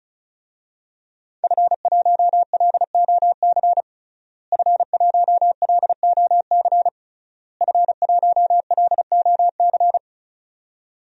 Morse code, heard as a keyed tone, about 36 wpm, 700 Hz. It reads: F1LOC F1LOC F1LOC
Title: DXCC-100 Call Signs - 5 Characters - Repeated 3x - 35wpm